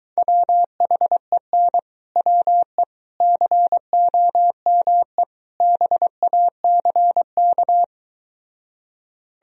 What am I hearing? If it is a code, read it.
Morse code, heard as a keyed tone, about 23 wpm, 700 Hz. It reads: WHEN WE COME BACK